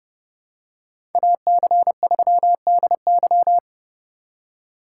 Morse code, heard as a keyed tone, about 28 wpm, 700 Hz. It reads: AC3DY